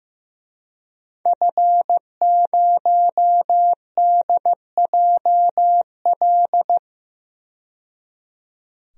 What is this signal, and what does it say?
Morse code, heard as a keyed tone, about 15 wpm, 700 Hz. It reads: F0DJL
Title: DXCC-100 Call Signs - 5 Characters - 15wpm